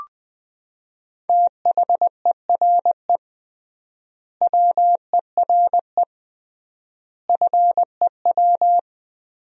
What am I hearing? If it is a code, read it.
Morse code, heard as a keyed tone, about 20 wpm, 700 Hz. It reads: THERE WERE FEW
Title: 3x Word-Spacing - Sentences from Top 300 Words - 20wpm